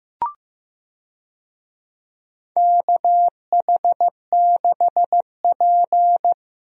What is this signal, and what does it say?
Morse code, heard as a keyed tone, about 15 wpm, 700 Hz. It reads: KH6P
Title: DXCC-100 Call Signs - 4 Characters - Repeated 3x - 15wpm